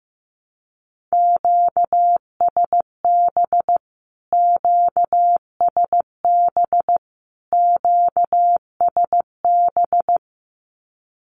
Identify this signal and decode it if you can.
Morse code, heard as a keyed tone, about 14 wpm, 700 Hz. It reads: QSB QSB QSB
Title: Q-Codes Repeated 3x 15wpm